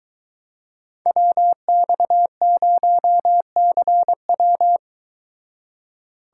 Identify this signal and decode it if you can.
Morse code, heard as a keyed tone, about 22 wpm, 700 Hz. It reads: WX0CW